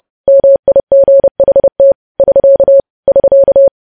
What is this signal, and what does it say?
Morse code, heard as a keyed tone, about 30 wpm, 550 Hz. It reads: MIGHT <SK> <SK>
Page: news headlines in morse code at 30 wpm